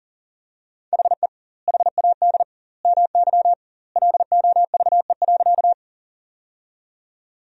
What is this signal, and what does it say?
Morse code, heard as a keyed tone, about 40 wpm, 700 Hz. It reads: HE HAD MY LOVE.
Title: 2x Word-Spacing - Sentences from Top 400 Words - 40wpm